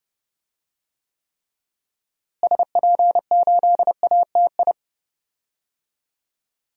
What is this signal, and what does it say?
Morse code, heard as a keyed tone, about 30 wpm, 700 Hz. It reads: SP8ATI
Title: DXCC-100 Call Signs - 6 Characters - Repeated 3x - 30wpm